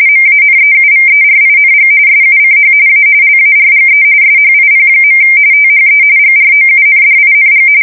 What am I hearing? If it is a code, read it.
RTTY, about 45.45 baud. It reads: MANEUVERING SUCH A LARGE SHIP. THE GROUNDING
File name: fsk170 / USS-Missouri